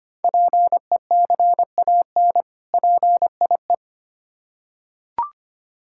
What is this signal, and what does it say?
Morse code, heard as a keyed tone, about 25 wpm, 700 Hz. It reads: PECAN PIE
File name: practice / 25wpm / Difficult Words with Hints - 25wpm